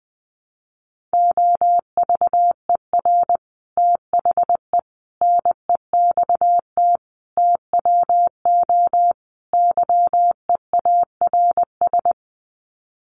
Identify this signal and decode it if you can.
Morse code, heard as a keyed tone, about 20 wpm, 700 Hz. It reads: OVER THE NEXT TWO YEARS
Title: Sets of 5 Words Encore2 20wpm